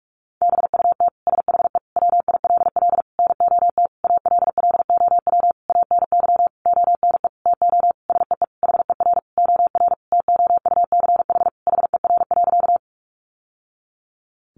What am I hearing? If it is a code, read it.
Morse code, heard as a keyed tone, about 44 wpm, 700 Hz. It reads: BUT SHE WILL NOT ALLOW ANY ONE TO SEE HER OR TOUCH HER.
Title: Sentences from Top 1200 Words 45wpm